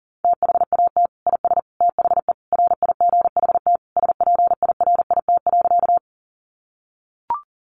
Morse code, heard as a keyed tone, about 37 wpm, 700 Hz. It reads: THAT IS THE RIGHT SPIRIT.
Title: Sentences from Top 1400 Words 40wpm